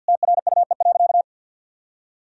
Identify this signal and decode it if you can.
Morse code, heard as a keyed tone, about 50 wpm, 700 Hz. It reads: TRUE.
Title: Sentences from Top 1000 Words - Review Part 4 - 50wpm